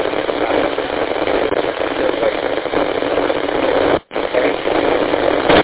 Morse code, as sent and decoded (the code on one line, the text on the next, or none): none